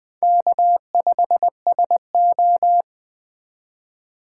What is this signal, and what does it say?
Morse code, heard as a keyed tone, about 20 wpm, 700 Hz. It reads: K5SO